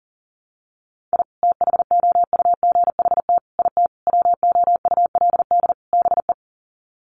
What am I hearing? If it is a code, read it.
Morse code, heard as a keyed tone, about 40 wpm, 700 Hz. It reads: I THOUGHT IT WOULD BE
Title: Sets of 5 Words Encore 40wpm